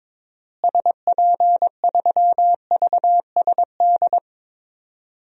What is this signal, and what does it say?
Morse code, heard as a keyed tone, about 22 wpm, 700 Hz. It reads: SP3VSD
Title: DXCC-100 Call Signs - 6 Characters - Repeated 3x - 22wpm